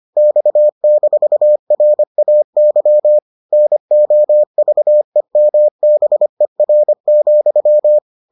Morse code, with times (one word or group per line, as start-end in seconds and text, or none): 0.2-3.2 X-RAY
3.5-8.0 NOVEMBER,